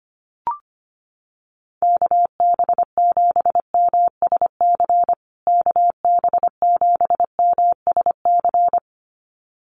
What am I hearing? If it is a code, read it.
Morse code, heard as a keyed tone, about 25 wpm, 700 Hz. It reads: KB7MSC KB7MSC
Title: USA Call Signs Repeated 2x 25wpm